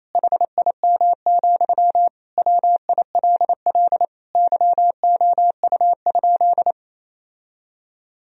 Morse code, heard as a keyed tone, about 27 wpm, 700 Hz. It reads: HIM, WILL YOU?
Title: Sentences from Top 1000 Words - Review Part 1 - 28wpm